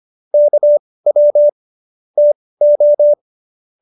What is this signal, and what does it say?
Morse code, heard as a keyed tone, about 25 wpm, spaced as for 18 wpm, 590 Hz. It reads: KW TO
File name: qso201b 18